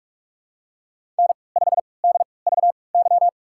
Morse code, heard as a keyed tone, about 45 wpm, spaced as for 25 wpm, 700 Hz. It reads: N5DVY
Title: USA Call Signs - 3x Character Spacing - 45wpm